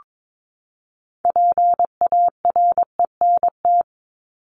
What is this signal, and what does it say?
Morse code, heard as a keyed tone, about 22 wpm, 700 Hz. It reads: PARENT
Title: Top 1751-2000 Words 22wpm